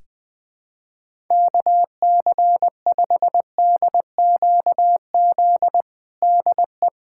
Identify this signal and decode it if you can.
Morse code, heard as a keyed tone, about 20 wpm, 700 Hz. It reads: KC5DQZ DE
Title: USA Callsign DE Callsign 20wpm